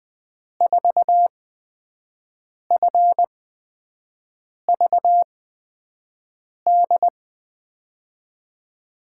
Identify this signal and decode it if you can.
Morse code, heard as a keyed tone, about 21 wpm, spaced as for 5 wpm, 700 Hz. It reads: 4FVD
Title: USA Call Signs - 6 Characters - 8x Character Spacing - 20wpm